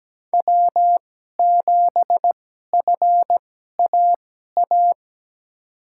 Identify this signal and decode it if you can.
Morse code, heard as a keyed tone, about 17 wpm, spaced as for 12 wpm, 700 Hz. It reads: W7FAA
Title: USA Call Signs - 5 Characters - 2x Character Spacing - 17wpm